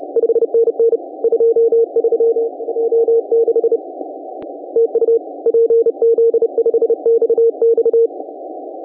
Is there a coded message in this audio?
Yes: Morse code from 0.2 to 8.1 s, heard as a keyed tone, about 31 wpm, 460 Hz. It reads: 5NN 23 O6 TU PZ5XX